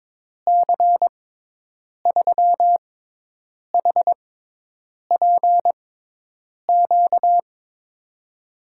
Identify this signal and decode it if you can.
Morse code, heard as a keyed tone, about 21 wpm, spaced as for 8 wpm, 700 Hz. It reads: C3HPQ
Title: USA Call Signs - 6 Characters - 6x Character Spacing - 22wpm